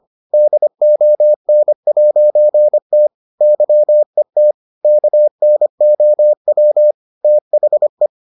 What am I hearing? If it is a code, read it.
Morse code, heard as a keyed tone, about 25 wpm, 600 Hz. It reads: DON'T YET KNOW THE